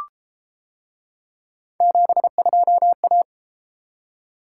Morse code, heard as a keyed tone, about 33 wpm, 700 Hz. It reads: 72A